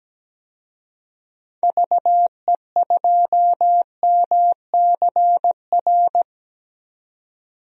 Morse code, heard as a keyed tone, about 17 wpm, 700 Hz. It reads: VE2MCR
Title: DXCC-100 Call Signs - 6 Characters - 17wpm